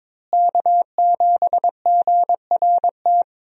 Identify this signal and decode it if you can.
Morse code, heard as a keyed tone, about 21 wpm, 690 Hz. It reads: K7GRT
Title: USA Call Signs - 5 Characters - Repeated 3x - 22wpm